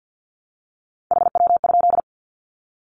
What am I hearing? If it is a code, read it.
Morse code, heard as a keyed tone, about 50 wpm, 700 Hz. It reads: HW?
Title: Common QSO Phrases 50wpm